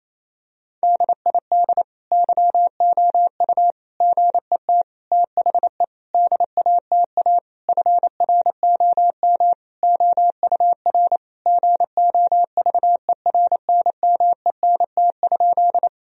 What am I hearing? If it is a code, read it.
Morse code, heard as a keyed tone, about 28 wpm, 700 Hz. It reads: DID YOU GET THE DATA FROM OUR GOVERNMENT?
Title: Sentences from Top 1000 Words - Review Part 3 - 28wpm